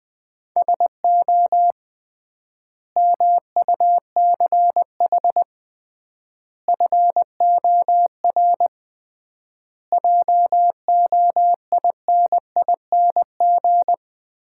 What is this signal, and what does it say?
Morse code, heard as a keyed tone, about 20 wpm, 700 Hz. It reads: SO MUCH FOR JOINING
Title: 3x Word-Spacing - Sets of 5 Words - 20wpm